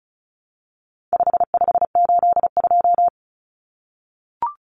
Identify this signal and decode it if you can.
Morse code, heard as a keyed tone, about 34 wpm, 700 Hz. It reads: 5582
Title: Numbers - 4-Digit 35wpm